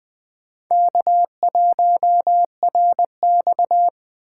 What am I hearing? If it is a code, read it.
Morse code, heard as a keyed tone, about 23 wpm, 700 Hz. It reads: K1RX
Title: Speed Racing - USA 1x2 Call Signs 20wpm